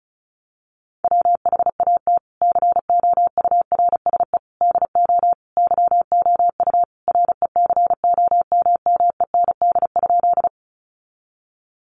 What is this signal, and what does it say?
Morse code, heard as a keyed tone, about 34 wpm, 700 Hz. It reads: WHAT COURSE DO YOU RECOMMEND?
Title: Sentences from Top 1200 Words 35wpm